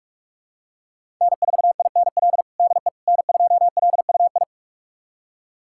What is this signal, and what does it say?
Morse code, heard as a keyed tone, about 47 wpm, 700 Hz. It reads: N4INL DE N2LUI
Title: USA Callsign DE Callsign - 5 Characters - 45wpm